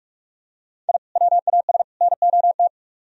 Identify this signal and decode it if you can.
Morse code, heard as a keyed tone, about 45 wpm, 700 Hz. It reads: I WAS NOT